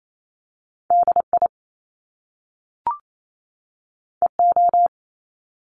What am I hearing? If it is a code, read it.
Morse code, heard as a keyed tone, about 28 wpm, 700 Hz. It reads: DI EO
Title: Top Two Letter Combinations in Words 28wpm